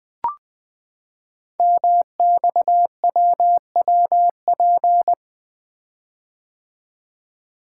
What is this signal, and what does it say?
Morse code, heard as a keyed tone, about 20 wpm, 700 Hz. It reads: MXWWP
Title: Code Groups - 5 Letters - 20wpm